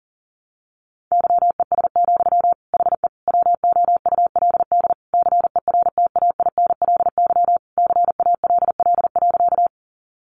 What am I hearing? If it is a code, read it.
Morse code, heard as a keyed tone, about 40 wpm, 700 Hz. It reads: YES, HE WOULD CERTAINLY CALL.
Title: Sentences from Top 900 Words 40wpm